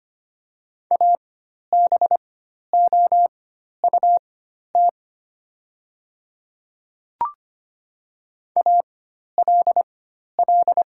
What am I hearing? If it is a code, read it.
Morse code, heard as a keyed tone, about 25 wpm, spaced as for 12 wpm, 700 Hz. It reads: ABOUT ALL
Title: 4x Character Spacing - Top 100 Words - 25wpm